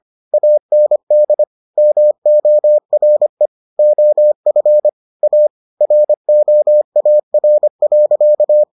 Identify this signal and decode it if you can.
Morse code, heard as a keyed tone, about 25 wpm, 600 Hz. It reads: AND MORE OF A ROAR.